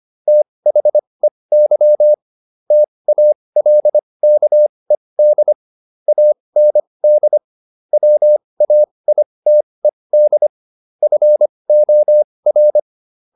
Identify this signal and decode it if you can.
Morse code, heard as a keyed tone, about 25 wpm, spaced as for 20 wpm, 600 Hz. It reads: THEY TALKED AND WAITED FOR